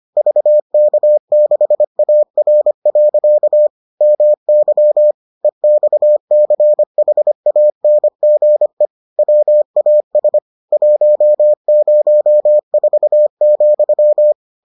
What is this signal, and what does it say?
Morse code, heard as a keyed tone, about 25 wpm, 600 Hz. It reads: VK6AR. MY EXCHANGE WAS 104,